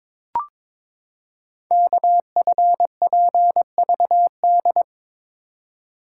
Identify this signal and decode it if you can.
Morse code, heard as a keyed tone, about 22 wpm, 700 Hz. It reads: KFPVD
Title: Code Groups - 5 Letters - 22wpm